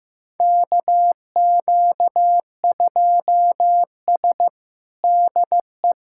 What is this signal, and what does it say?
Morse code, heard as a keyed tone, about 15 wpm, 700 Hz. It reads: KQ2S DE
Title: USA Callsign DE Callsign - 4 Characters - 15wpm